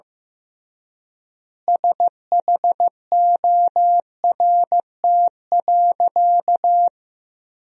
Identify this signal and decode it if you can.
Morse code, heard as a keyed tone, about 15 wpm, 700 Hz. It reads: SHORT.
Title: 3x Word-Spacing - Sentences from Top 500 Words - 15wpm